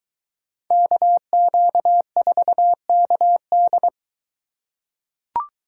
Morse code, heard as a keyed tone, about 22 wpm, 700 Hz. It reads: KQ4KD